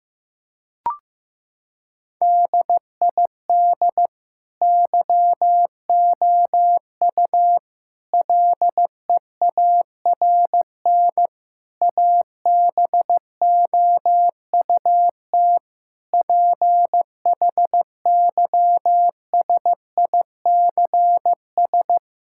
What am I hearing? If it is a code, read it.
Morse code, heard as a keyed tone, about 15 wpm, 700 Hz. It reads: DID YOU LEARN ABOUT PHYSICS
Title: Sentences from Top 1000 Words - Review Part 5 - 15wpm